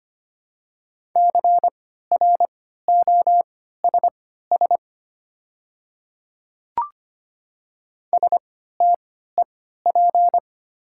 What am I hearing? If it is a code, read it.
Morse code, heard as a keyed tone, about 25 wpm, spaced as for 14 wpm, 690 Hz. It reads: CROSS STEP
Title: 3x Character Spacing - Top 500 Words - 25wpm